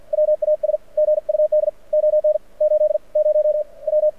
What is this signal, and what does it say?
Morse code, heard as a keyed tone, about 57 wpm, spaced as for 43 wpm, 600 Hz. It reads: WAS GUD ON 80W
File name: archived / CRSnet